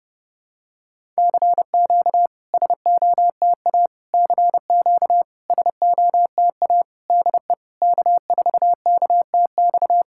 Morse code, heard as a keyed tone, about 30 wpm, 690 Hz. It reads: CQ SOTA CQ SOTA DE K4KTX
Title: SOTA QSO Practice 30wpm